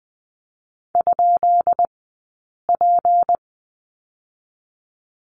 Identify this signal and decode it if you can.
Morse code, heard as a keyed tone, about 20 wpm, spaced as for 8 wpm, 700 Hz. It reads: ?P
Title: 151 Lesson 25d - Letter p - Two Characters 20wpm